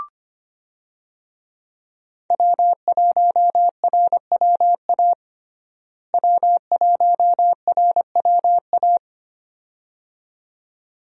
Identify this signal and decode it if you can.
Morse code, heard as a keyed tone, about 25 wpm, 700 Hz. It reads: W1RWA W1RWA
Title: USA Call Signs - 5 Characters - Repeated 2x - 25wpm